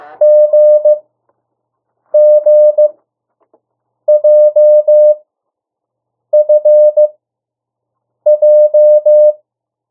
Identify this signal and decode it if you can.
Morse code, heard as a keyed tone, about 15 wpm, spaced as for 6 wpm, 600 Hz. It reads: GGJFJ
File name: morse mp3 / 02 urok 1